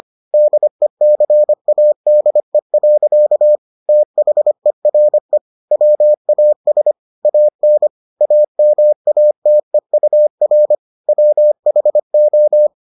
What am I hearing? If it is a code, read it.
Morse code, heard as a keyed tone, about 25 wpm, 600 Hz. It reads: DECADE. THERE WAS AN AMATEUR WHO